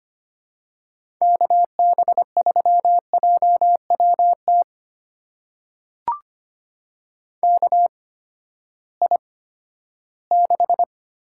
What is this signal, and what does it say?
Morse code, heard as a keyed tone, about 25 wpm, spaced as for 14 wpm, 700 Hz. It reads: KB3JWT KI6
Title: USA Call Signs - 6 Characters - 8x Character Spacing - 25wpm